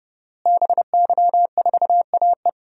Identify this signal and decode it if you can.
Morse code, heard as a keyed tone, about 31 wpm, 700 Hz. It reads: BY4AE
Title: DXCC-100 Call Signs - 5 Characters - Repeated 2x - 30wpm